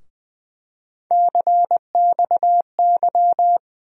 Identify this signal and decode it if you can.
Morse code, heard as a keyed tone, about 20 wpm, 700 Hz. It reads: CXY
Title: USA Call Sign Postfixes - 3 Letters 20wpm